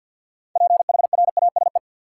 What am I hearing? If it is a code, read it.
Morse code, heard as a keyed tone, about 50 wpm, 700 Hz. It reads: PHRASE
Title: Six Letter Words - 50wpm